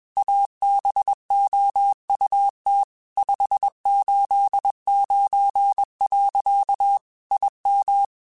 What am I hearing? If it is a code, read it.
Morse code, heard as a keyed tone, about 21 wpm, 800 Hz. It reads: ABOUT 589. IM
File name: QSOs / QSOs-mp3s / WD0FBT 1@20